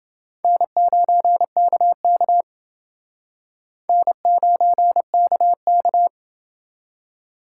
Morse code, heard as a keyed tone, about 30 wpm, 700 Hz. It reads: N9KK N9KK